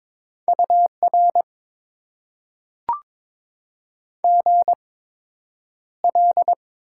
Morse code, heard as a keyed tone, about 22 wpm, spaced as for 9 wpm, 700 Hz. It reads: UR GL
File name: Top 90 QSO Elements - 8x Character Spacing - 22wpm